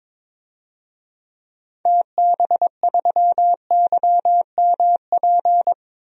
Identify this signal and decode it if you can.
Morse code, heard as a keyed tone, about 22 wpm, 700 Hz. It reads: TB3YMP
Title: DXCC-100 Call Signs - 6 Characters - 22wpm